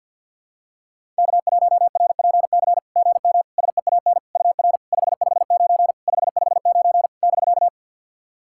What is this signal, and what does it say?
Morse code, heard as a keyed tone, about 52 wpm, 690 Hz. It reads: K1RPC GM SEAN UR 559 559 <BK>
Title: SOTA QSO Practice 50wpm